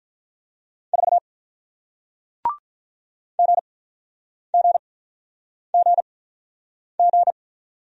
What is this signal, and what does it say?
Morse code, heard as a keyed tone, about 48 wpm, spaced as for 9 wpm, 700 Hz. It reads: 4 GGGG